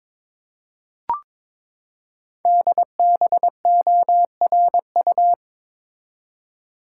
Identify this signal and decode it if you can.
Morse code, heard as a keyed tone, about 22 wpm, 700 Hz. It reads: DBORU
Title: Code Groups - 5 Letters - 22wpm